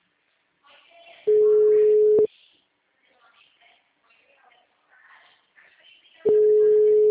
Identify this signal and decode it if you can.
Ringback tone (Europe, 425 Hz), with speech.